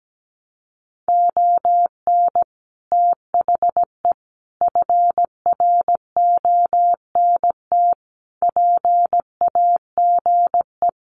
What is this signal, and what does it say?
Morse code, heard as a keyed tone, about 17 wpm, 700 Hz. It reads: ON THE FRONT PAGE
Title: Sets of 5 Words Encore 17wpm